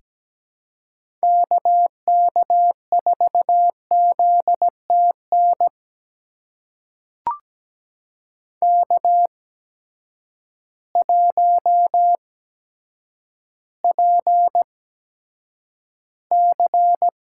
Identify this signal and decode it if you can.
Morse code, heard as a keyed tone, about 17 wpm, spaced as for 6 wpm, 700 Hz. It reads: KK4ZT NK1PC
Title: USA Call Signs - 8x Character Spacing - 17wpm